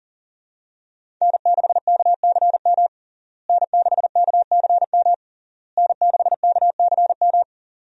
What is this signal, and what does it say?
Morse code, heard as a keyed tone, about 37 wpm, 700 Hz. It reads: N6KCM N6KCM N6KCM
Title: USA Call Signs - 5 Characters - Repeated 3x - 40wpm